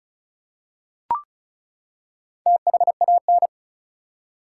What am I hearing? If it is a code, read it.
Morse code, heard as a keyed tone, about 36 wpm, 700 Hz. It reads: THAN
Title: Four Letter Words - 35wpm